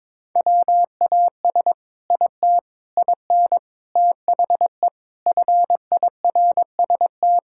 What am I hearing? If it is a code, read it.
Morse code, heard as a keyed tone, about 22 wpm, 700 Hz. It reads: WAS IT IN THE FIRST
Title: Sentences from Top 1000 Words - Review Part 5 - 22wpm